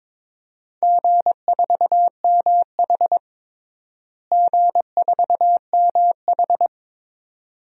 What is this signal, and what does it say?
Morse code, heard as a keyed tone, about 22 wpm, 700 Hz. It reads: G4MH G4MH
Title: DXCC-100 Call Signs - 4 Characters - Repeated 2x - 22wpm